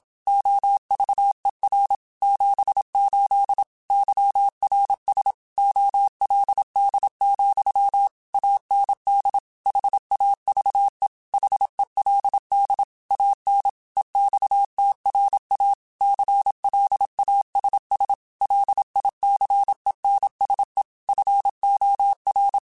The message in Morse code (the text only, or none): OVER 78 YRS OLD, AND HAVE HELD AN EXTRA CLASS LICENSE FOR